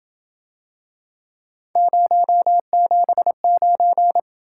Morse code, heard as a keyed tone, about 27 wpm, 700 Hz. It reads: 079